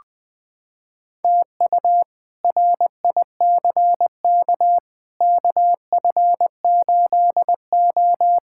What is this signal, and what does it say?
Morse code, heard as a keyed tone, about 20 wpm, 700 Hz. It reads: TU RICK KF8O
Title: K1USN SST QSO Practice - 20wpm